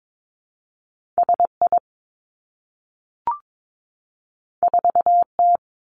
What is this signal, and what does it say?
Morse code, heard as a keyed tone, about 22 wpm, 700 Hz. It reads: S I 4 T
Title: Two Challenging Letter-Number Combinations 22wpm